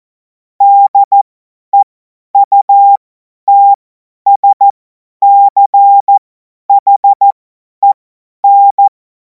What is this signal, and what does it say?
Morse code, heard as a keyed tone, about 14 wpm, spaced as for 10 wpm, 800 Hz. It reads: DEUTSCHEN
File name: Morsekurs / Witze / Witz 28. April 2026 50 F70